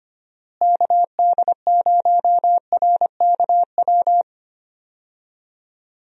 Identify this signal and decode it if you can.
Morse code, heard as a keyed tone, about 25 wpm, 700 Hz. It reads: KD0RKW
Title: USA Call Signs - 6 Characters - Repeated 3x - 25wpm